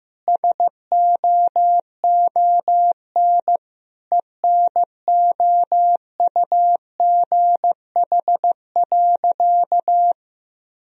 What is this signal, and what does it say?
Morse code, heard as a keyed tone, about 15 wpm, 700 Hz. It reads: SOON ENOUGH.
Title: Sentences from Top 1000 Words - Review Part 4 - 15wpm